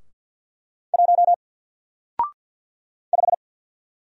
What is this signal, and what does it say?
Morse code, heard as a keyed tone, about 48 wpm, 700 Hz. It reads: 1 5
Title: Single Character (Letter:Number:Punctuation:Prosign) Alt 50wpm